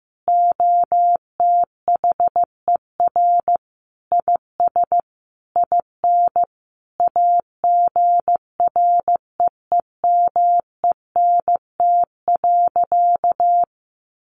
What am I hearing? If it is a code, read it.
Morse code, heard as a keyed tone, about 14 wpm, 700 Hz. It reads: OTHER IS IN AGREEMENT.
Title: Sentences from Top 1500 Words 15wpm